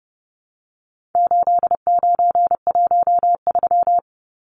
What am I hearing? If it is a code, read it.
Morse code, heard as a keyed tone, about 30 wpm, 700 Hz. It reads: 8913
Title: Numbers - 4-Digit 30wpm